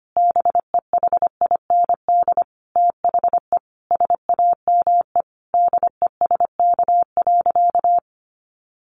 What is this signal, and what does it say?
Morse code, heard as a keyed tone, about 25 wpm, 700 Hz. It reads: BEHIND THE SAME DESK.
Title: A Sound Of Thunder 25wpm